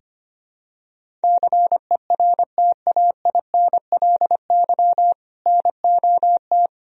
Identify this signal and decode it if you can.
Morse code, heard as a keyed tone, about 25 wpm, 700 Hz. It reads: CERTAINLY NOT
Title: Sentences from Top 1000 Words - Review Part 1 - 25wpm